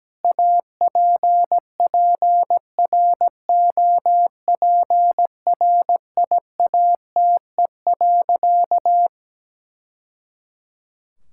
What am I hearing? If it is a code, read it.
Morse code, heard as a keyed tone, about 17 wpm, 700 Hz. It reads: APPROPRIATE.
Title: Sentences from Top 1000 Words - Review Part 5 - 17wpm